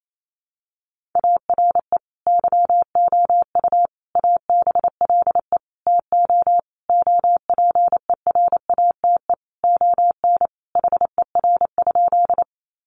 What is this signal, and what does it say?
Morse code, heard as a keyed tone, about 28 wpm, 700 Hz. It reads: ARE YOU ABLE TO OPERATE ON HER?
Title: Sentences from Top 1500 Words 28wpm